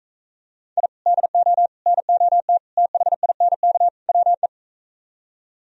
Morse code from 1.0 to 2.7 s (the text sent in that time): DO NOT